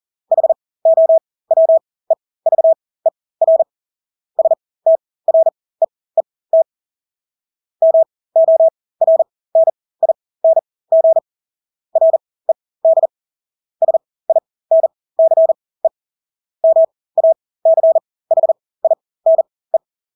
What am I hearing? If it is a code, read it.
Morse code, heard as a keyed tone, about 40 wpm, spaced as for 20 wpm, 650 Hz. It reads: HOWEVER STREET MORNING RED SINCE MACHINE
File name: Engelske ord 40-20 wpm